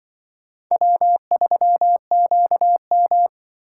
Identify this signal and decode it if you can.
Morse code, heard as a keyed tone, about 23 wpm, 700 Hz. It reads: W3QM
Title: Speed Racing - USA Call Signs 20wpm